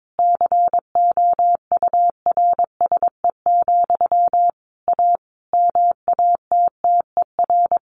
Morse code, heard as a keyed tone, about 22 wpm, 700 Hz. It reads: COURSE, A MATTER
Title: Sentences from Top 1500 Words 22wpm